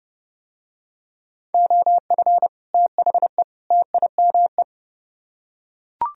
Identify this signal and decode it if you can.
Morse code, heard as a keyed tone, about 30 wpm, 700 Hz. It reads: OF THE TIME